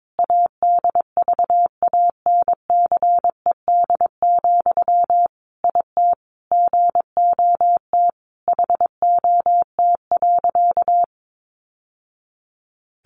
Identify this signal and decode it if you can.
Morse code, heard as a keyed tone, about 22 wpm, 700 Hz. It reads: ADVANCED, IT GOT HOT.
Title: Sentences from Top 1200 Words 22wpm